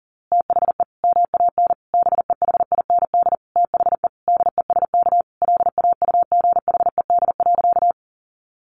Morse code, heard as a keyed tone, about 40 wpm, 700 Hz. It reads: THE MAN BEHIND THE DESK LAUGHED.
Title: A Sound Of Thunder 40wpm